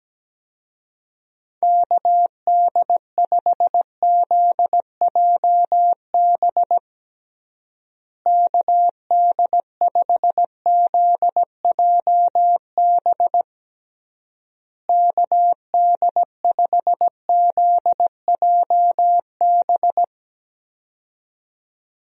Morse code, heard as a keyed tone, about 17 wpm, 700 Hz. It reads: KD5ZJB KD5ZJB KD5ZJB
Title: USA Call Signs - 6 Characters - Repeated 3x - 17wpm